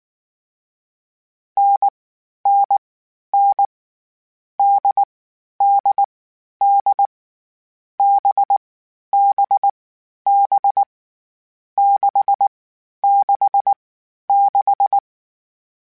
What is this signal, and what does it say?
Morse code, heard as a keyed tone, about 19 wpm, 800 Hz. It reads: N N N D D D B B B 6 6 6
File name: NDB NDB6 @18